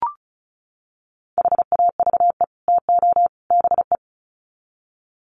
Morse code, heard as a keyed tone, about 34 wpm, 700 Hz. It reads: HAVE TO BE
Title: Sets of 3 Words 35wpm